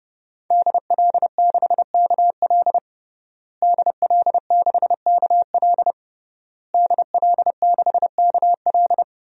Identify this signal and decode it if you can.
Morse code, heard as a keyed tone, about 30 wpm, 700 Hz. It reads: DL6KL DL6KL DL6KL
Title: DXCC-100 Call Signs - 5 Characters - Repeated 3x - 30wpm